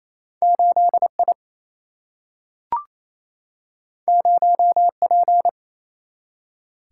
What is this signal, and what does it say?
Morse code, heard as a keyed tone, about 28 wpm, 700 Hz. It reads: 8 I 0 P
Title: Single-Letter Single-Number 28wpm